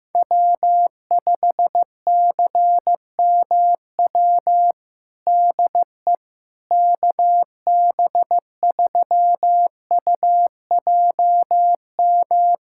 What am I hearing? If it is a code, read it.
Morse code, heard as a keyed tone, about 15 wpm, 700 Hz. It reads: W5CMW DE KB3UJM
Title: USA Callsign DE Callsign 15wpm